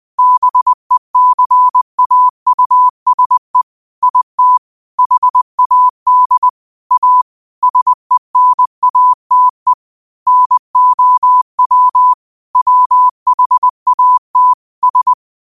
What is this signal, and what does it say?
Morse code, heard as a keyed tone, about 20 wpm, 1000 Hz. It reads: BECAUSE IT HAD A SENATE NOW WHAT S
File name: qotdfiles / 20wpm